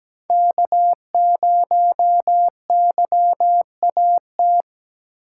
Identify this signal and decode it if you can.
Morse code, heard as a keyed tone, about 17 wpm, 700 Hz. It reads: K0YAT
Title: USA Call Signs - 5 Characters - 3x Character Spacing - 17wpm